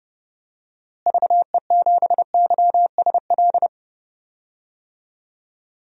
Morse code, heard as a keyed tone, about 31 wpm, 700 Hz. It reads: VE7YSL
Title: DXCC-100 Call Signs - 6 Characters - Repeated 2x - 30wpm